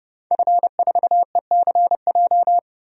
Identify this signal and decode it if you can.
Morse code, heard as a keyed tone, about 30 wpm, 700 Hz. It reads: F4ECJ
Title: DXCC-100 Call Signs - 5 Characters - Repeated 2x - 30wpm